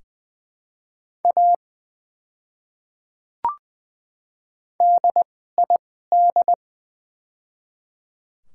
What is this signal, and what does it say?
Morse code, heard as a keyed tone, about 20 wpm, spaced as for 15 wpm, 700 Hz. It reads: A DID
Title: 2x Character Spacing - Top 100 Words - 20wpm